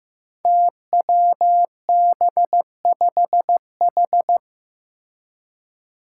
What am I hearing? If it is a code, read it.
Morse code, heard as a keyed tone, about 15 wpm, 700 Hz. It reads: TWB5H
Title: Five Challenging Letter-Number Combinations 15wpm